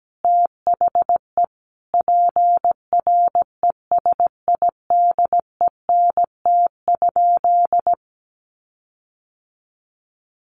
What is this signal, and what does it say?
Morse code, heard as a keyed tone, about 17 wpm, 700 Hz. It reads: THE PRESIDENT?
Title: Sentences from Top 1300 Words 17wpm